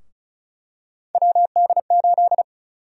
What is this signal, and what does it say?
Morse code, heard as a keyed tone, about 35 wpm, 700 Hz. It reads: WD8
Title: USA Call Sign Prefixes - Repeated 2x - 35wpm